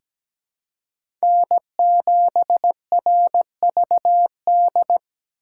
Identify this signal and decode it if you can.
Morse code, heard as a keyed tone, about 17 wpm, 700 Hz. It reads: N7RVD
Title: K1USN SST QSO Practice - 17wpm